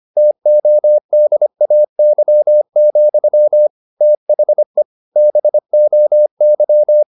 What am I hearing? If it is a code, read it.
Morse code, heard as a keyed tone, about 25 wpm, 600 Hz. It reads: TODAY, THE BOY
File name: the old man and the sea part 1 25wpm